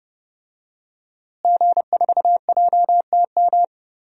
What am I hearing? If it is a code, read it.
Morse code, heard as a keyed tone, about 31 wpm, 700 Hz. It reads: G4JTM
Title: DXCC-100 Call Signs - 5 Characters - Repeated 3x - 30wpm